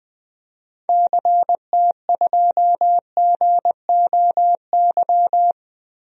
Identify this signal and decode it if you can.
Morse code, heard as a keyed tone, about 21 wpm, 700 Hz. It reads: CT2GOY
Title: DXCC-100 Call Signs - 6 Characters - Repeated 2x - 20wpm